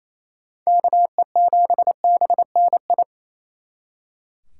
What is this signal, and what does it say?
Morse code, heard as a keyed tone, about 33 wpm, 700 Hz. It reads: KE7BNI
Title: Speed Racing - USA 2x3 Call Signs 28wpm